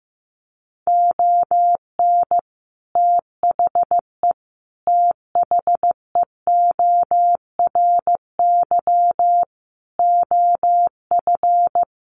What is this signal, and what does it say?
Morse code, heard as a keyed tone, about 15 wpm, 700 Hz. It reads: ON THE THEORY OF
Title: Sentences from Top 1400 Words 15wpm